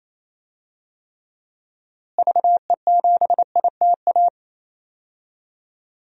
Morse code, heard as a keyed tone, about 28 wpm, 700 Hz. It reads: VE7ITA
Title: DXCC-100 Call Signs - 6 Characters - Repeated 2x - 28wpm